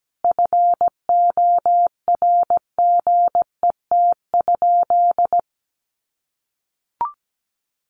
Morse code, heard as a keyed tone, about 17 wpm, 700 Hz. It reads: FORGET?
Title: Sentences from Top 1200 Words 17wpm